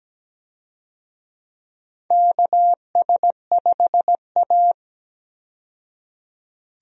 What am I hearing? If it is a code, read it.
Morse code, heard as a keyed tone, about 17 wpm, 700 Hz. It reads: KS5A
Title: USA Call Signs - 4 Characters - 17wpm